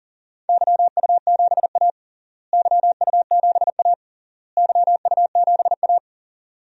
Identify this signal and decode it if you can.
Morse code, heard as a keyed tone, about 40 wpm, 700 Hz. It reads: YU7A YU7A YU7A
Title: DXCC-100 Call Signs - 4 Characters - Repeated 3x - 40wpm